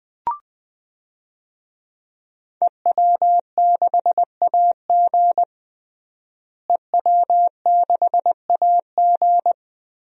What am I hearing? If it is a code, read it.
Morse code, heard as a keyed tone, about 19 wpm, 700 Hz. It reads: EW6AG EW6AG
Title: DXCC-100 Call Signs - 5 Characters - Repeated 3x - 20wpm